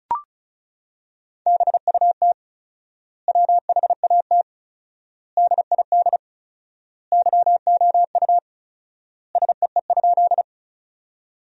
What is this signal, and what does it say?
Morse code, heard as a keyed tone, about 34 wpm, 700 Hz. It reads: BUT WHAT DID YOU SEE?
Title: 4x Word-Spacing - Sentences from Top 100 Words - 35wpm